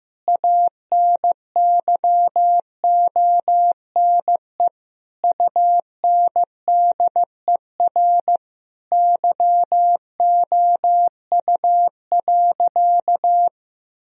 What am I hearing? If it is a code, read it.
Morse code, heard as a keyed tone, about 15 wpm, 700 Hz. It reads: ANYONE UNDER YOU.
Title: Sentences from Top 1000 Words - Review Part 1 - 15wpm